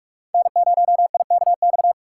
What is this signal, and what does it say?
Morse code, heard as a keyed tone, about 60 wpm, spaced as for 53 wpm, 700 Hz. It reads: N0IKX
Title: Speed Racing - USA 1x3 Call Signs 45wpm